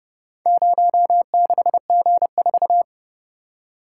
Morse code, heard as a keyed tone, about 30 wpm, 700 Hz. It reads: 06G4